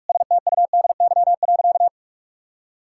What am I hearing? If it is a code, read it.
Morse code, heard as a keyed tone, about 45 wpm, 700 Hz. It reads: STUDY.
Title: Sentences from Top 1000 Words - Review Part 5 - 45wpm